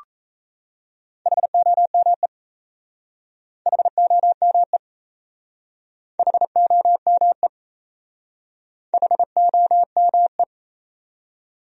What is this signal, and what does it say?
Morse code, heard as a keyed tone, about 37 wpm, 700 Hz. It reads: HOME HOME HOME HOME